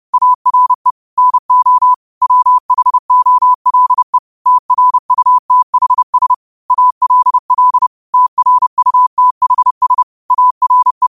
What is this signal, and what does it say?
Morse code, heard as a keyed tone, about 30 wpm, 1000 Hz. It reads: ARE NO WHOLE TRUTHS ALL TRUTHS ARE